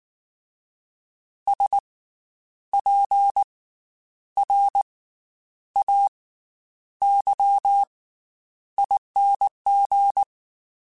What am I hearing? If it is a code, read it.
Morse code, heard as a keyed tone, about 19 wpm, spaced as for 9 wpm, 800 Hz. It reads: SPRAYING
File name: Special INGs / Special INGs-mp3s / Slow ing @18